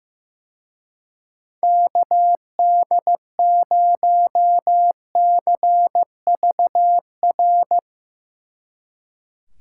Morse code, heard as a keyed tone, about 15 wpm, 700 Hz. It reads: KD0CVR
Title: Speed Racing - USA 2x3 Call Signs 15wpm